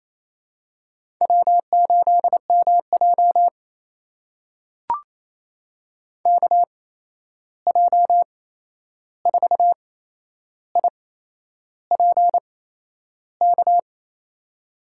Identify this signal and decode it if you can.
Morse code, heard as a keyed tone, about 28 wpm, spaced as for 11 wpm, 700 Hz. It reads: W8MJ KJ4IPK
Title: USA Call Signs - 8x Character Spacing - 28wpm